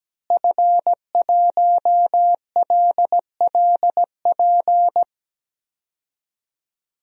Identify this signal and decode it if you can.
Morse code, heard as a keyed tone, about 17 wpm, 700 Hz. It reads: F1LLP
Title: DXCC-100 Call Signs - 5 Characters - Repeated 2x - 17wpm